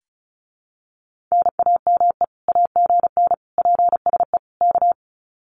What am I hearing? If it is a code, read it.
Morse code, heard as a keyed tone, about 34 wpm, 700 Hz. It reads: NAME AGN PSE K
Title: Common QSO Phrases Encore 35wpm